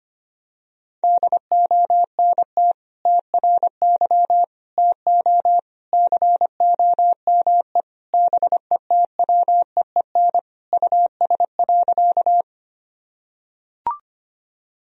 Morse code, heard as a keyed tone, about 25 wpm, 700 Hz. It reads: DONT TRY TO COME BETWEEN US.
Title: Sentences from Top 1000 Words - Review Part 1 - 25wpm